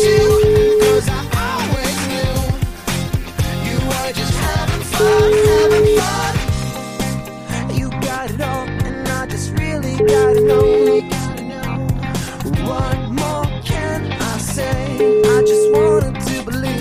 Ringback tone (Europe, 425 Hz), with noise.